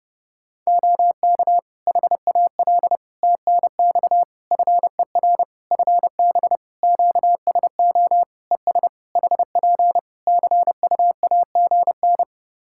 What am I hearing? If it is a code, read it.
Morse code, heard as a keyed tone, about 30 wpm, 700 Hz. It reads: OK HAL TNX FER FB QSO ES HP CUAGN
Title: Practice QSO - Exchange 4 - Ending - 30wpm